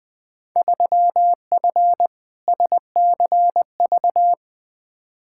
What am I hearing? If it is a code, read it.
Morse code, heard as a keyed tone, about 20 wpm, 700 Hz. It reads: 3F SCV
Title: Field Day Exchanges - 20wpm